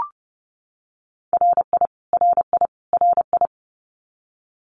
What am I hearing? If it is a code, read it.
Morse code, heard as a keyed tone, about 30 wpm, 700 Hz. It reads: RI RI RI
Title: US States Repeated 3x 30wpm